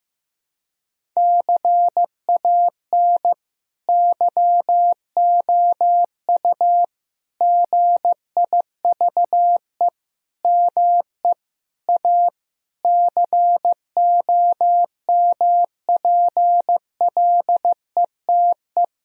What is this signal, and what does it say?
Morse code, heard as a keyed tone, about 15 wpm, 700 Hz. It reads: CAN YOU GIVE ME A COMPLETE
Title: Sentences from Top 1000 Words - Review Part 2 - 15wpm